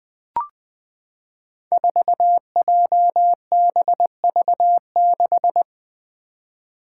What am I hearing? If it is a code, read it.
Morse code, heard as a keyed tone, about 20 wpm, 700 Hz. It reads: 4JBV6
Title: Five Challenging Letter-Number Combinations 20wpm